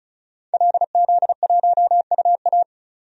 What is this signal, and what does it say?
Morse code, heard as a keyed tone, about 34 wpm, 700 Hz. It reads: LZ1UA
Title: DXCC-100 Call Signs - 5 Characters - 35wpm